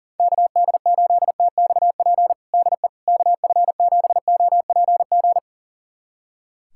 Morse code, heard as a keyed tone, about 40 wpm, 700 Hz. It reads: KD8TXP DE KF7OPG
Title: USA Callsign DE Callsign - 6 Characters - 40wpm